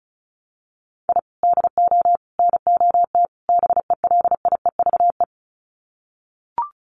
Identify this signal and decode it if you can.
Morse code, heard as a keyed tone, about 34 wpm, 700 Hz. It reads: I DO NOT BELIEVE
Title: Sets of 4 Words 35wpm